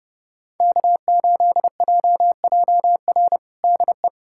Morse code, heard as a keyed tone, about 32 wpm, 700 Hz. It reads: K8JJR DE
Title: USA Callsign DE Callsign - 5 Characters - 30wpm